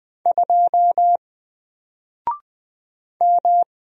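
Morse code, heard as a keyed tone, about 20 wpm, 700 Hz. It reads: 2 M
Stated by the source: Single letter and number Morse drill at 20 wpm, from N4LCD.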